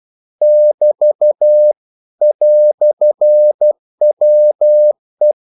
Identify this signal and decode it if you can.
Morse code, heard as a keyed tone, about 12 wpm, 600 Hz. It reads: <BT> "WE
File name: CW / Practice / EnglishQuotations0112